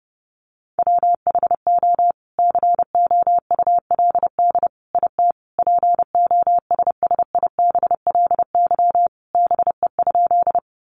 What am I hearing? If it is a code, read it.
Morse code, heard as a keyed tone, about 28 wpm, 700 Hz. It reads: WHO COULD IT POSSIBLY BE?
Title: Sentences from Top 1300 Words 30wpm